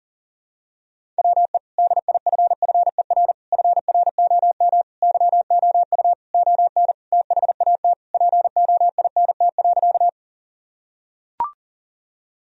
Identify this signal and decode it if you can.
Morse code, heard as a keyed tone, about 37 wpm, 700 Hz. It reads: WE DIFFER FROM YOU ON THAT POINT.
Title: Sentences from Top 1000 Words - Review Part 1 - 40wpm